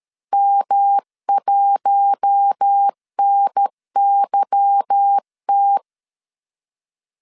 Morse code, heard as a keyed tone, about 13 wpm, 800 Hz. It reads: M1NYT
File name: LEZIONI / 9 2 12wpm 2301